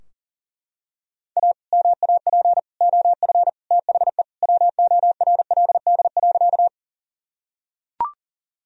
Morse code, heard as a keyed tone, about 37 wpm, 700 Hz. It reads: A MAP OF THE WORLD.
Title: Sentences from Top 1000 Words - Review Part 2 - 40wpm